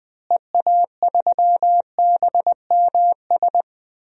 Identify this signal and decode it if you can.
Morse code, heard as a keyed tone, about 20 wpm, 700 Hz. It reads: EA3BMS
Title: DXCC-100 Call Signs - 6 Characters - Repeated 2x - 20wpm